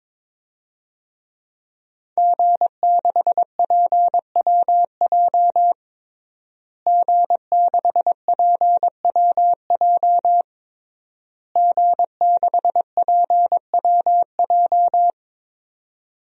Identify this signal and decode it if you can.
Morse code, heard as a keyed tone, about 22 wpm, 700 Hz. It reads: G6PWJ G6PWJ G6PWJ
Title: DXCC-100 Call Signs - 5 Characters - Repeated 3x - 22wpm